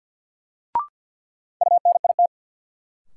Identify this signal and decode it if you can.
Morse code, heard as a keyed tone, about 51 wpm, 700 Hz. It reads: UNIT
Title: Sentences from Top 1000 Words - Review Part 2 - New Word Drill - 50wpm